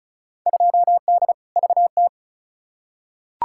Morse code, heard as a keyed tone, about 34 wpm, 700 Hz. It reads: 2D VT
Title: Field Day Exchanges Encore - 35wpm